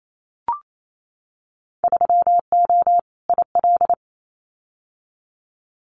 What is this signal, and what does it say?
Morse code, heard as a keyed tone, about 28 wpm, 700 Hz. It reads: 3O IL
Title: Winter Field Day Exchanges 28wpm